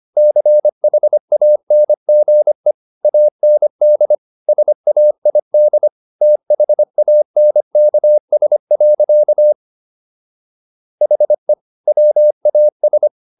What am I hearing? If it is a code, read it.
Morse code, heard as a keyed tone, about 25 wpm, 600 Hz. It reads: CHANGE AND SAID THANKS. HE WAS